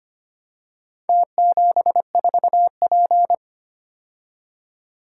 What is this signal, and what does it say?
Morse code, heard as a keyed tone, about 25 wpm, 700 Hz. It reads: T74P
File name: practice / 25wpm / Four Letter-Number Combinations - 25wpm